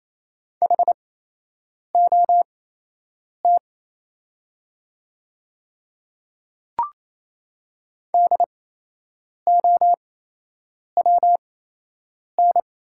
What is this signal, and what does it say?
Morse code, heard as a keyed tone, about 28 wpm, spaced as for 8 wpm, 700 Hz. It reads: HOT DOWN
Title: 8x Character Spacing - Top 100 Words - 28wpm